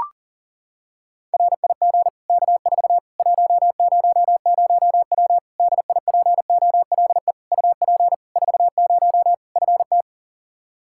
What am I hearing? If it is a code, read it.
Morse code, heard as a keyed tone, about 40 wpm, 700 Hz. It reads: RIG K4 100W DIPOLE UP 40 FT
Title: Practice QSO - Exchange 2 - Info - 40wpm